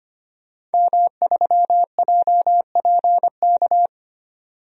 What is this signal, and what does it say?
Morse code, heard as a keyed tone, about 25 wpm, 700 Hz. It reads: M3JPK
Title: DXCC-100 Call Signs - 5 Characters - 25wpm